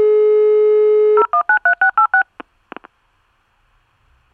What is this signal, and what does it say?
DTMF, heard as telephone touch tones, tones of about 100 ms. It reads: 0193686